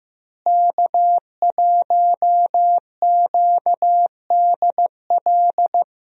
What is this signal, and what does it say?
Morse code, heard as a keyed tone, about 15 wpm, 700 Hz. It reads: K1QDL
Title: USA Call Signs - 5 Characters - Repeated 3x - 15wpm